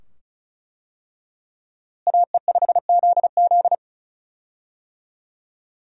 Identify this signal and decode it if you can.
Morse code, heard as a keyed tone, about 34 wpm, 700 Hz. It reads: AE5ZZ
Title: USA Call Signs - 5 Characters - Repeated 2x - 35wpm